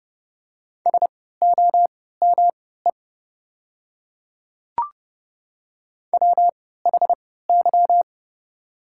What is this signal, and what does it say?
Morse code, heard as a keyed tone, about 30 wpm, spaced as for 17 wpm, 690 Hz. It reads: SOME WHY
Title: 3x Character Spacing - Top 200 Words - 30wpm